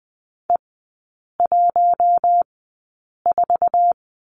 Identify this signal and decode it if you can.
Morse code, heard as a keyed tone, about 20 wpm, spaced as for 8 wpm, 700 Hz. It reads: E14
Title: 067 Lesson 12e - Letter d - Three Characters 20wpm